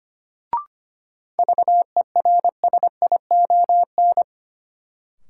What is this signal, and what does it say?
Morse code, heard as a keyed tone, about 25 wpm, 700 Hz. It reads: VERSION
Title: Sentences from Top 1400 Words - New Word Drill Alt - 25wpm